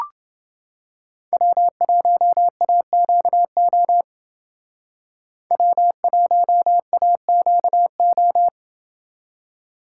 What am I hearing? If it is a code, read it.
Morse code, heard as a keyed tone, about 28 wpm, 700 Hz. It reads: W1AQO W1AQO